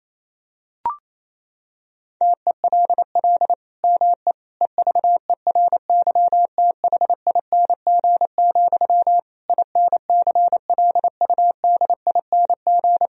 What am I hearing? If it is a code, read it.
Morse code, heard as a keyed tone, about 28 wpm, 700 Hz. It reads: TELL ME EVERYTHING, INCLUDING
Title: Sentences from Top 1000 Words - Review Part 4 - 28wpm